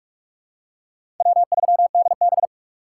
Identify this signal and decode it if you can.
Morse code, heard as a keyed tone, about 47 wpm, 700 Hz. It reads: W3DB
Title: USA Call Signs - 4x Character Spacing - 45wpm